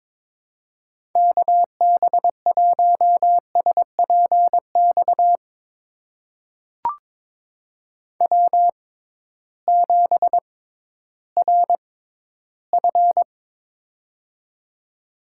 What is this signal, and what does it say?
Morse code, heard as a keyed tone, about 22 wpm, spaced as for 13 wpm, 700 Hz. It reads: KB1SPX W7RF
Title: USA Call Signs - 6x Character Spacing - 22wpm